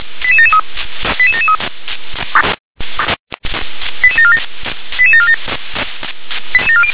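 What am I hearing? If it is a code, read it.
Five-tone selective call (ZVEI), heard as quick runs of five tones, tones of about 80 ms. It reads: 88773 88773 88747 88747
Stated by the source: ZVEI five-tone selective call received by SDR.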